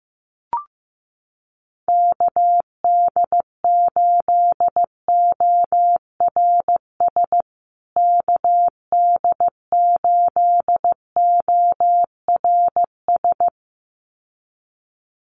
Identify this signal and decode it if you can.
Morse code, heard as a keyed tone, about 15 wpm, 700 Hz. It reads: KD8ORS KD8ORS
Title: USA Call Signs Repeated 2x 15wpm